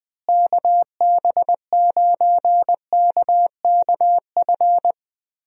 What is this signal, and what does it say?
Morse code, heard as a keyed tone, about 21 wpm, 700 Hz. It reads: KB9KKF
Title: USA Call Signs - 6 Characters - Repeated 2x - 20wpm